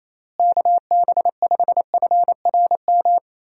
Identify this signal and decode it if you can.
Morse code, heard as a keyed tone, about 28 wpm, 700 Hz. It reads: KB5FRM
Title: USA Call Signs - 6 Characters - Repeated 2x - 28wpm